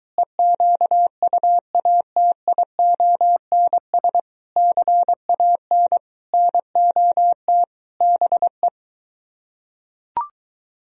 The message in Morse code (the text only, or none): EQUATIONS CAN NOT BE